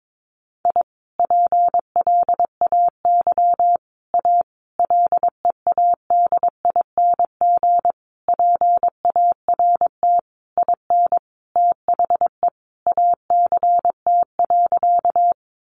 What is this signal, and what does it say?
Morse code, heard as a keyed tone, about 22 wpm, 700 Hz. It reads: I PLAY A LEADING PART IN THE ACT.
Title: Sentences from Top 1100 Words 22wpm